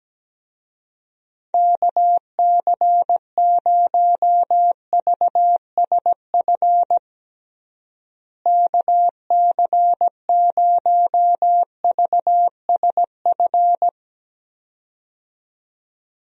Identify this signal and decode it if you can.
Morse code, heard as a keyed tone, about 17 wpm, 700 Hz. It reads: KC0VSF KC0VSF
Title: USA Call Signs - 6 Characters - Repeated 2x - 17wpm